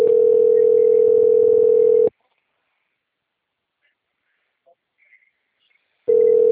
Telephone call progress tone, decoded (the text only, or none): ringback tone